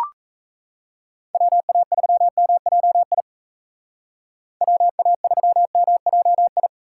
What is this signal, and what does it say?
Morse code, heard as a keyed tone, about 37 wpm, 700 Hz. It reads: WA3MJI WA3MJI